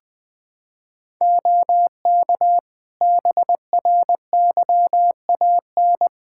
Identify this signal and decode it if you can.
Morse code, heard as a keyed tone, about 20 wpm, 700 Hz. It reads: OK BRYAN
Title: Practice QSO - Exchange 2 - Info - 20wpm